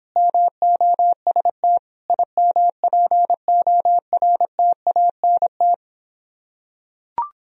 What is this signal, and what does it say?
Morse code, heard as a keyed tone, about 26 wpm, 700 Hz. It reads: MOST IMPORTANT